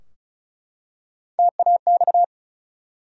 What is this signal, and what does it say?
Morse code, heard as a keyed tone, about 36 wpm, 700 Hz. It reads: TAX
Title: Three Letter Words - 35wpm